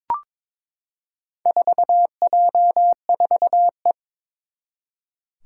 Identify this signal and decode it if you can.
Morse code, heard as a keyed tone, about 22 wpm, 700 Hz. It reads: 4J4E
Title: Four Challenging Letter-Number Combinations 22wpm